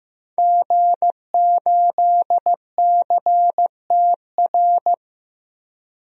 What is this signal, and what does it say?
Morse code, heard as a keyed tone, about 15 wpm, 700 Hz. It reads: G8CTR
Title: DXCC-100 Call Signs - 5 Characters - 15wpm